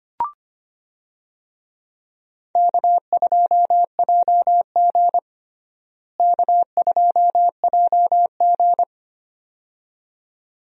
Morse code, heard as a keyed tone, about 25 wpm, 700 Hz. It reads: K2JG K2JG
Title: USA Call Signs - 4 Characters - Repeated 2x - 25wpm